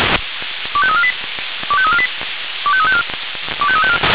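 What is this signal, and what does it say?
Five-tone selective call (ZVEI), heard as quick runs of five tones, tones of about 70 ms. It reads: 25348 25348 25354 25354
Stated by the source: ZVEI five-tone selective call received on SDR.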